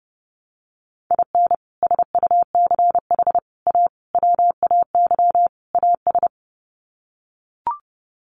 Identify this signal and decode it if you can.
Morse code, heard as a keyed tone, about 30 wpm, 700 Hz. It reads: IN SUCH A WAY AS
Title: Sets of 5 Words Encore 30wpm